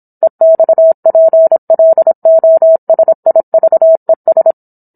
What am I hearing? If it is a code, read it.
Morse code, heard as a keyed tone, about 27 wpm, 650 Hz. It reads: EXPLOSIVES
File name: Book One / 17-Chapter 17 - 26 WPM